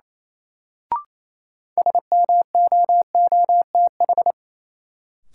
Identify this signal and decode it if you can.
Morse code, heard as a keyed tone, about 28 wpm, 700 Hz. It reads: SMOOTH
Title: Sentences from Top 1400 Words - New Word Drill - 28wpm